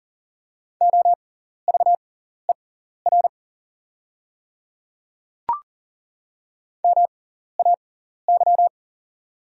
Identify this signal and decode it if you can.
Morse code, heard as a keyed tone, about 40 wpm, spaced as for 14 wpm, 690 Hz. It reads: OVER MAY
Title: 6x Character Spacing - Top 300 Words - 40wpm